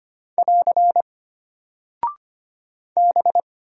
Morse code, heard as a keyed tone, about 25 wpm, 700 Hz. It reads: <AR> B
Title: Single Character (Letter:Number:Punctuation:Prosign) Alt 25wpm